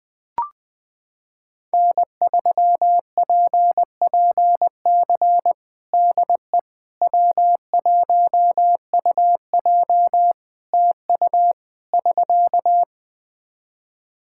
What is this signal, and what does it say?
Morse code, heard as a keyed tone, about 20 wpm, 700 Hz. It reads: N3PPC DE W1UJ TU <SK>
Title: Practice QSO - Exchange 4 - Ending - 20wpm